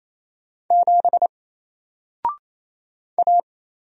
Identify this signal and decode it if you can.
Morse code, heard as a keyed tone, about 27 wpm, 700 Hz. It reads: 7 A
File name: Single-Letter-Number-28wpm